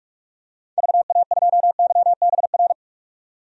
Morse code, heard as a keyed tone, about 41 wpm, 700 Hz. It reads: VA2YBR
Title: DXCC-100 Call Signs - 6 Characters - 45wpm